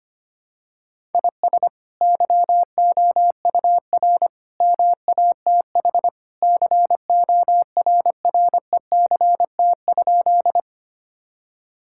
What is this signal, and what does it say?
Morse code, heard as a keyed tone, about 25 wpm, 700 Hz. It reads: IS YOUR MATH CORRECT?
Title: Sentences from Top 1000 Words - Review Part 6 - 25wpm